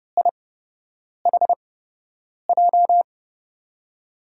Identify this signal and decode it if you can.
Morse code, heard as a keyed tone, about 30 wpm, spaced as for 8 wpm, 700 Hz. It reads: IHJ
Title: USA Call Signs - 6 Characters - 8x Character Spacing - 30wpm